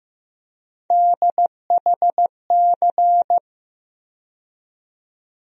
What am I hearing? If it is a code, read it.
Morse code, heard as a keyed tone, about 15 wpm, 700 Hz. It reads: DHC
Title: USA Call Sign Postfixes - 3 Letters 15wpm